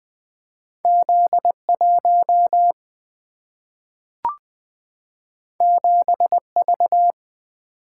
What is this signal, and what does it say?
Morse code, heard as a keyed tone, about 20 wpm, 700 Hz. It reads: Z1 7V
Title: Single-Letter Single-Number 20wpm